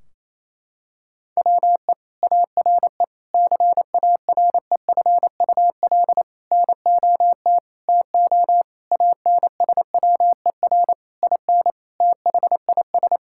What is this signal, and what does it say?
Morse code, heard as a keyed tone, about 28 wpm, 700 Hz. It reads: WE ARE CAREFUL NOT TO ANSWER IN THIS
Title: Sentences from Top 1600 Words 28wpm